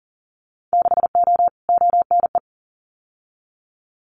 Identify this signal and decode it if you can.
Morse code, heard as a keyed tone, about 41 wpm, 700 Hz. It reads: 6O ONE
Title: Winter Field Day Exchanges Encore 40wpm